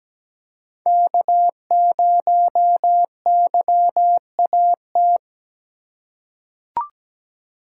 Morse code, heard as a keyed tone, about 17 wpm, 700 Hz. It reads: K0YAT
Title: USA Call Signs - 5 Characters - 2x Character Spacing - 17wpm